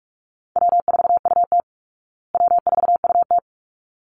Morse code, heard as a keyed tone, about 44 wpm, 700 Hz. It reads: W4UT W4UT
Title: Easy Call Signs Repeated 2x - 45wpm